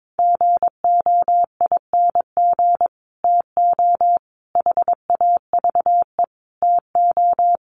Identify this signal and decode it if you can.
Morse code, heard as a keyed tone, about 22 wpm, 700 Hz. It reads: GOING TO HAVE TO
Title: Sets of 5 Words Encore 22wpm